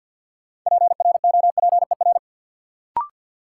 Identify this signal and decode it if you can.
Morse code, heard as a keyed tone, about 51 wpm, 700 Hz. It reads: PROPER
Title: Sentences from Top 800 Words - New Word Drill - 50wpm